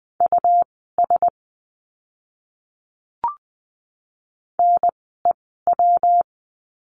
Morse code, heard as a keyed tone, about 20 wpm, spaced as for 15 wpm, 700 Hz. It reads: US NEW
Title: 160 Lesson 26f - Letter g - Words 20wpm